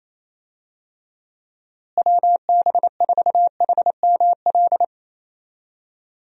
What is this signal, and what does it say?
Morse code, heard as a keyed tone, about 27 wpm, 700 Hz. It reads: WB4HML
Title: USA Call Signs - 6 Characters - Repeated 2x - 28wpm